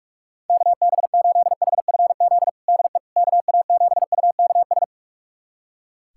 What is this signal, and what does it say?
Morse code, heard as a keyed tone, about 46 wpm, 690 Hz. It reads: KB8HFZ DE KA7UKS
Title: USA Callsign DE Callsign - 6 Characters - 45wpm